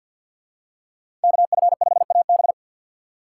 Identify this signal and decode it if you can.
Morse code, heard as a keyed tone, about 53 wpm, 700 Hz. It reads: KF5AB
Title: USA Call Signs - 2x Character Spacing - 50wpm